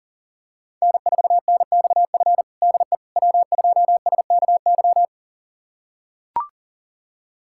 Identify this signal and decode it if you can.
Morse code, heard as a keyed tone, about 40 wpm, 700 Hz. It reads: N4NXF DE W2SKY
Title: USA Callsign DE Callsign - 5 Characters - 40wpm